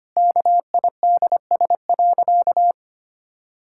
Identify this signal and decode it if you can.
Morse code, heard as a keyed tone, about 25 wpm, 700 Hz. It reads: KIDS.
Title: Pangram Challenge v2 - 25wpm